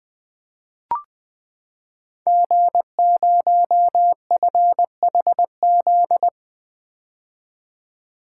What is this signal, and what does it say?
Morse code, heard as a keyed tone, about 20 wpm, 700 Hz. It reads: G0FHZ
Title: Code Groups - 5 Letter:Numbers - 20wpm